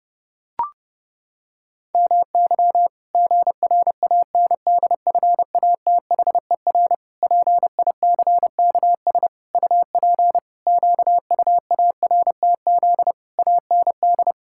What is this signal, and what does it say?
Morse code, heard as a keyed tone, about 30 wpm, 700 Hz. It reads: MY GRANDFATHER PICKS UP QUARTZ AND
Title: Pangram Challenge v2 - 30wpm